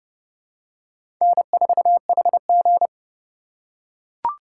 Morse code, heard as a keyed tone, about 30 wpm, 700 Hz. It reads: N4HG
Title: USA Call Signs - 4 Characters - 4x Character Spacing - 30wpm